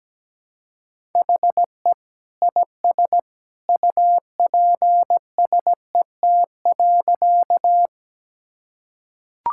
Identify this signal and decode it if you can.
Morse code, heard as a keyed tone, about 17 wpm, 700 Hz. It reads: HE IS UPSET.
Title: Sentences from Top 1000 Words - Review Part 5 - 17wpm